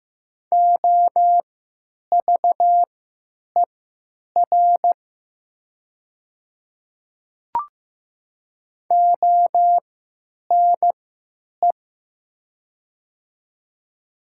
Morse code, heard as a keyed tone, about 15 wpm, spaced as for 9 wpm, 700 Hz. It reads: OVER ONE
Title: 3x Character Spacing - Top 300 Words - 15wpm